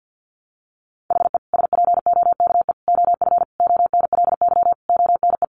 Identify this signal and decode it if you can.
Morse code, heard as a keyed tone, about 50 wpm, 700 Hz. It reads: HE SPOKE OF ONLY ONE
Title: Sentences from Top 1200 Words 50wpm